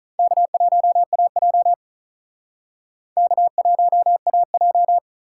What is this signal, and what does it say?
Morse code, heard as a keyed tone, about 47 wpm, 700 Hz. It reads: K1AJ K1AJ